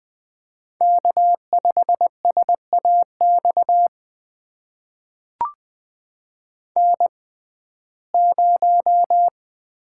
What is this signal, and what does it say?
Morse code, heard as a keyed tone, about 20 wpm, spaced as for 15 wpm, 700 Hz. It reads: K5SAX N0
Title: USA Call Signs - 5 Characters - 6x Character Spacing - 20wpm